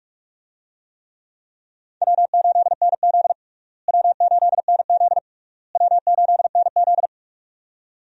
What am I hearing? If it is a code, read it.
Morse code, heard as a keyed tone, about 46 wpm, 700 Hz. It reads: W8NZ W8NZ W8NZ
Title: USA Call Signs - 4 Characters - Repeated 3x - 45wpm